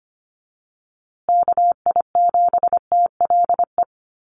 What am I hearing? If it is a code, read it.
Morse code, heard as a keyed tone, about 25 wpm, 700 Hz. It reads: KI7TLE
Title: USA Call Signs Repeated 2x 25wpm